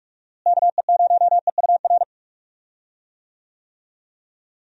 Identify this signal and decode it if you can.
Morse code, heard as a keyed tone, about 43 wpm, 700 Hz. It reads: KE0EUR
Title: USA Call Signs - 6 Characters - Repeated 2x - 45wpm